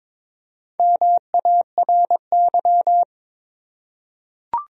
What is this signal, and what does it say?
Morse code, heard as a keyed tone, about 22 wpm, 700 Hz. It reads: MARY
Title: USA Common Names - Max 4 Letters - 22wpm